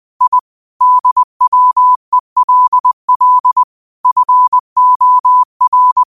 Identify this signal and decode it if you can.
Morse code, heard as a keyed tone, about 20 wpm, 1000 Hz. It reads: I DWELL FOR
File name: qotdfiles / 20wpm